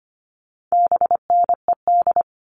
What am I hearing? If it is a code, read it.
Morse code, heard as a keyed tone, about 25 wpm, 700 Hz. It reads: BNED
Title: All Cut Numbers - 4-Digit 25wpm